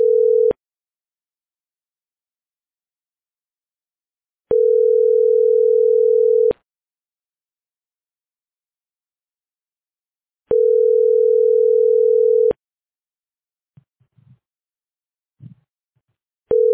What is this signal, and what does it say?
Ringback tone (North America), 440 + 480 Hz.